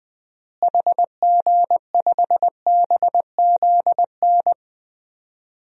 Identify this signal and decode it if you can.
Morse code, heard as a keyed tone, about 20 wpm, 700 Hz. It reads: HG5BZN
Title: DXCC-100 Call Signs - 6 Characters - Repeated 3x - 20wpm